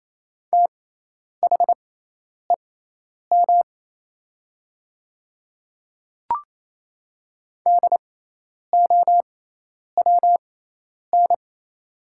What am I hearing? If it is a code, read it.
Morse code, heard as a keyed tone, about 28 wpm, spaced as for 10 wpm, 690 Hz. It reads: THEM DOWN
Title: 6x Character Spacing - Top 100 Words - 28wpm